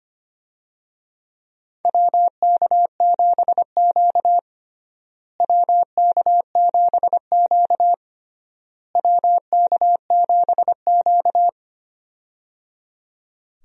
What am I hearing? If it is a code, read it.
Morse code, heard as a keyed tone, about 25 wpm, 700 Hz. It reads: WK7Q WK7Q WK7Q
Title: USA Call Signs - 4 Characters - Repeated 3x - 25wpm